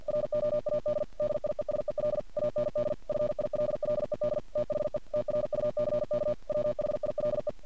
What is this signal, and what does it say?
Morse code, heard as a keyed tone, about 51 wpm, 600 Hz. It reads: ROAD DIESEL AND FILLED THE TRUCK WHILE
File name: archived / CRSnet